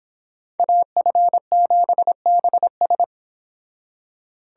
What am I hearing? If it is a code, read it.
Morse code, heard as a keyed tone, about 31 wpm, 700 Hz. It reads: AF7BS